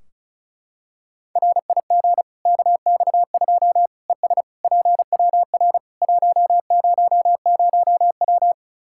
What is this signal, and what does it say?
Morse code, heard as a keyed tone, about 34 wpm, 700 Hz. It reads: RIG KX2 ES PWR 100W
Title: Practice QSO - Exchange 2 - Info - 35wpm